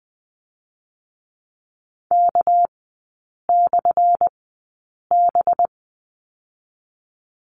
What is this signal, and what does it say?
Morse code, heard as a keyed tone, about 20 wpm, 700 Hz. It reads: K / B
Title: 198 Lesson 32e - Letter k - Three Characters 20wpm